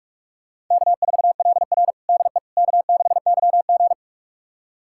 Morse code, heard as a keyed tone, about 42 wpm, 700 Hz. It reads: K4LR DE K6YG
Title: USA Callsign DE Callsign - 4 Characters - 45wpm